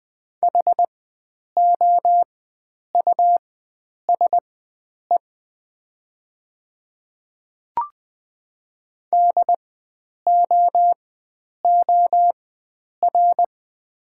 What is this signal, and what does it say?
Morse code, heard as a keyed tone, about 20 wpm, spaced as for 9 wpm, 700 Hz. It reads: HOUSE DOOR
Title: 4x Character Spacing - Top 500 Words - 20wpm